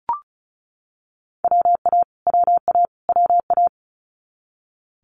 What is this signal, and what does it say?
Morse code, heard as a keyed tone, about 36 wpm, 700 Hz. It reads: WA WA WA
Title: Multi-letter Phonemes Repeated 3x 35wpm